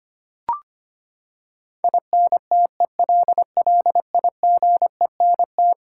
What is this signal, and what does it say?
Morse code, heard as a keyed tone, about 25 wpm, 700 Hz. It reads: INTELLIGENT
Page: morse-code-ninja-practice-25wpm